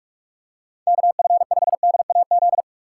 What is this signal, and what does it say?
Morse code, heard as a keyed tone, about 46 wpm, 700 Hz. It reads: KF5DAZ